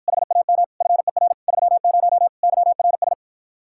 Morse code, heard as a keyed tone, about 52 wpm, 700 Hz. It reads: HAM FER 30 YRS